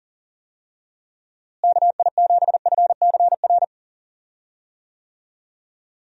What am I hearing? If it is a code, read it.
Morse code, heard as a keyed tone, about 37 wpm, 700 Hz. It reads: KI7FCR
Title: USA Call Signs - 6 Characters - Repeated 2x - 40wpm